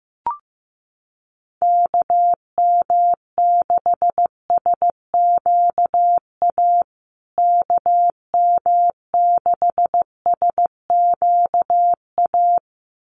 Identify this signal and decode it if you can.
Morse code, heard as a keyed tone, about 15 wpm, 700 Hz. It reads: KM6SQA KM6SQA
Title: USA Call Signs Repeated 2x 15wpm